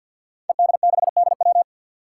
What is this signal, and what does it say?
Morse code, heard as a keyed tone, about 49 wpm, 700 Hz. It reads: ED6DW
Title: Five Challenging Letter-Number Combinations 50wpm